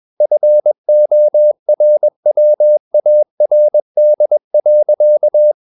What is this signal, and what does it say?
Morse code, heard as a keyed tone, about 21 wpm, 600 Hz. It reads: FORWARD.
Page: Romanen Never af Ken Follet på engelsk.